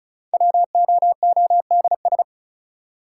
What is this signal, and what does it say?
Morse code, heard as a keyed tone, about 36 wpm, 700 Hz. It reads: WOODS
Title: Sentences from Top 1000 Words - Review Part 2 - New Word Drill - 35wpm